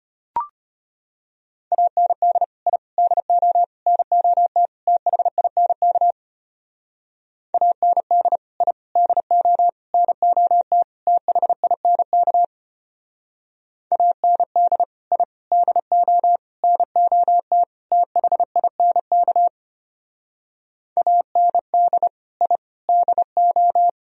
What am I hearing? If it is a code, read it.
Morse code, heard as a keyed tone, about 31 wpm, 700 Hz. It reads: AND I DO NOT THINK AND I DO NOT THINK AND I DO NOT THINK AND I DO
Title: Speed-Racing - Sets of 5 Words 25wpm